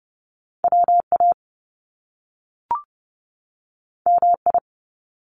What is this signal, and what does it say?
Morse code, heard as a keyed tone, about 30 wpm, 700 Hz. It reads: WA MI
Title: US States 30wpm